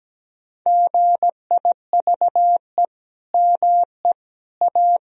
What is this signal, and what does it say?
Morse code, heard as a keyed tone, about 17 wpm, 700 Hz. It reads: GIVE ME A
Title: Sentences from Top 1000 Words - Review Part 4 - 17wpm